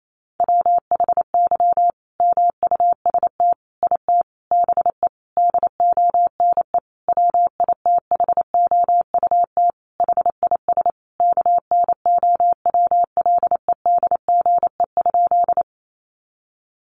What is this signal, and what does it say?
Morse code, heard as a keyed tone, about 28 wpm, 700 Hz. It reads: WHY MUST IT BE DONE WITHOUT HIS KNOWLEDGE?
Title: Sentences from Top 600 Words 28wpm